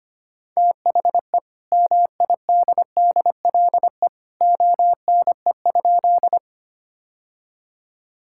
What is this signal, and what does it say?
Morse code, heard as a keyed tone, about 25 wpm, 700 Hz. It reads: THE MIDDLE ONE?
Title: Sentences from Top 1000 Words - Review Part 6 - 25wpm